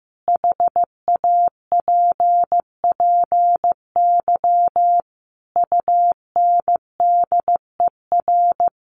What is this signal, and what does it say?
Morse code, heard as a keyed tone, about 15 wpm, 700 Hz. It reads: HAPPY UNDER
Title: Sentences from Top 1500 Words 15wpm